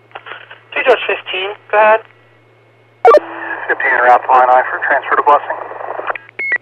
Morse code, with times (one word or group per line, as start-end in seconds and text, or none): none